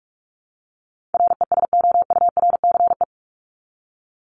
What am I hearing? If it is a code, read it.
Morse code, heard as a keyed tone, about 44 wpm, 700 Hz. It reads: RESOURCE
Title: Top 1001-1250 Words 45wpm